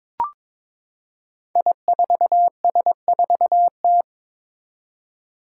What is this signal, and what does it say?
Morse code, heard as a keyed tone, about 22 wpm, 700 Hz. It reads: I4S4T
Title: Five Challenging Letter-Number Combinations 22wpm